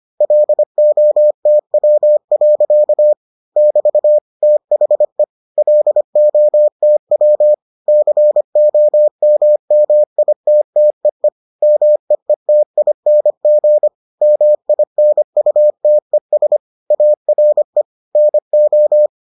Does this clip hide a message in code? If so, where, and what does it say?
Yes: Morse code from 0.2 to 19.1 s, heard as a keyed tone, about 25 wpm, 600 Hz. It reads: LOTW. <BT> THE LOTW COMMITTEE MEETING MINUTES ARE NO